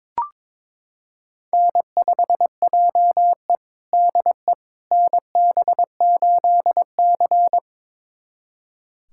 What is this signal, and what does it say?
Morse code, heard as a keyed tone, about 22 wpm, 700 Hz. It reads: N5JE DE NB8C
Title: USA Callsign DE Callsign - 4 Characters - 22wpm